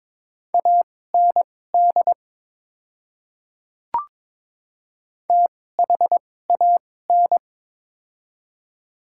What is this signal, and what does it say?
Morse code, heard as a keyed tone, about 22 wpm, spaced as for 16 wpm, 700 Hz. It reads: AND THAN
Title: 2x Character Spacing - Top 100 Words - 22wpm